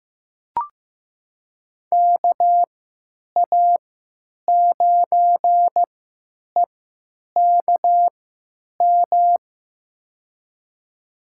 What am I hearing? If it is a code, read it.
Morse code, heard as a keyed tone, about 15 wpm, spaced as for 9 wpm, 700 Hz. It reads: KA9EKM
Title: USA Call Signs - 6 Characters - 3x Character Spacing - 15wpm